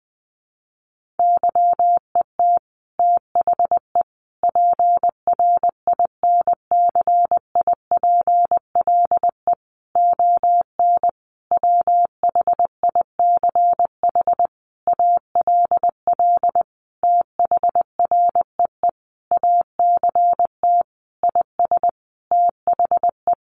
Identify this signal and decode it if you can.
Morse code, heard as a keyed tone, about 20 wpm, 700 Hz. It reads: YET THE PRINCIPLE ON WHICH ALL THREE ACT IS THE
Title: Sentences from Top 1400 Words 20wpm